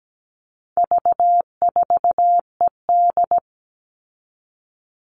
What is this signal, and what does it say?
Morse code, heard as a keyed tone, about 17 wpm, 700 Hz. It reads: V4ED
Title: All Cut Numbers - 4-Digit 17wpm